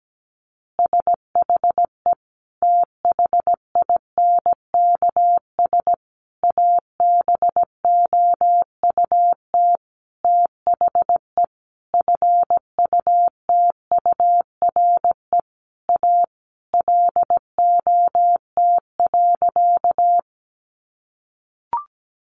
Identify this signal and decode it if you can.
Morse code, heard as a keyed tone, about 17 wpm, 700 Hz. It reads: SHE THINKS ABOUT THE FUTURE A LOT.
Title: Easy Phrases 17wpm